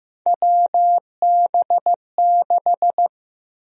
Morse code, heard as a keyed tone, about 15 wpm, 700 Hz. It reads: WB6
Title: USA Call Sign Prefixes - Repeated 2x - 15wpm